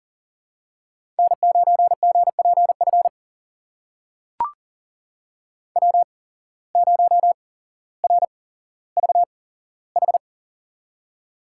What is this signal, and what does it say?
Morse code, heard as a keyed tone, about 38 wpm, spaced as for 17 wpm, 700 Hz. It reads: N9GPF W0RVH
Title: USA Call Signs - 5 Characters - 8x Character Spacing - 40wpm